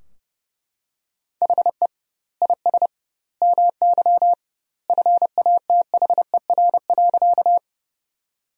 Morse code, heard as a keyed tone, about 32 wpm, 700 Hz. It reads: HE IS MY FATHER.
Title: 2x Word-Spacing - Sentences from Top 300 Words - 30wpm